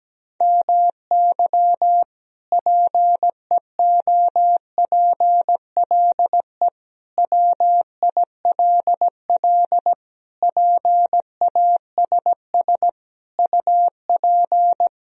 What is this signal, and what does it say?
Morse code, heard as a keyed tone, about 17 wpm, 700 Hz. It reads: MY PEOPLE WILL PASS UP
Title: Sentences from Top 1000 Words - Review Part 2 - 17wpm